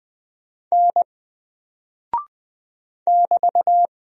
Morse code, heard as a keyed tone, about 20 wpm, 700 Hz. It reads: N <BT>
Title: Single Character (Letter:Number:Punctuation:Prosign) 20wpm